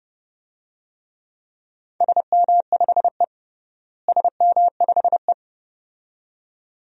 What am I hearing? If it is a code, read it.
Morse code, heard as a keyed tone, about 31 wpm, 700 Hz. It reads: SM5E SM5E
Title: DXCC-100 Call Signs - 4 Characters - Repeated 2x - 30wpm